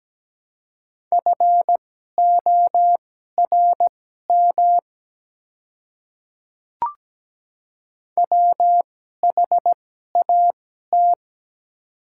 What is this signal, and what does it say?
Morse code, heard as a keyed tone, about 17 wpm, spaced as for 12 wpm, 700 Hz. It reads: FORM WHAT
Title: 2x Character Spacing - Top 200 Words - 17wpm